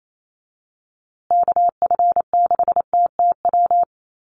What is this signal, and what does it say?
Morse code, heard as a keyed tone, about 28 wpm, 700 Hz. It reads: KF6TTW
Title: USA Call Signs Repeated 3x 28wpm